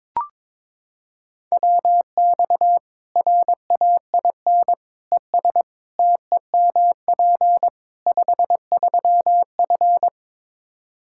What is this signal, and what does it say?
Morse code, heard as a keyed tone, about 22 wpm, 700 Hz. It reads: WX RAIN ES TEMP 53F
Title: Practice QSO - Exchange 2 - Info - 22wpm